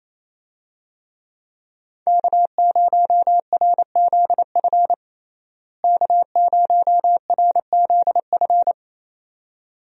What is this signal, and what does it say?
Morse code, heard as a keyed tone, about 28 wpm, 700 Hz. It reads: K0RZF K0RZF
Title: USA Call Signs - 5 Characters - Repeated 2x - 28wpm